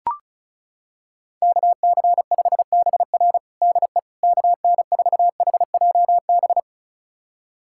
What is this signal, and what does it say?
Morse code, heard as a keyed tone, about 34 wpm, 700 Hz. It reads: KC5BR DE KN4HJB
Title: USA Callsign DE Callsign 35wpm